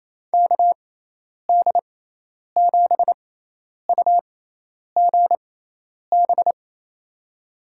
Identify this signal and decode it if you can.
Morse code, heard as a keyed tone, about 28 wpm, spaced as for 10 wpm, 700 Hz. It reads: KD7UGB
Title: USA Call Signs - 6 Characters - 6x Character Spacing - 28wpm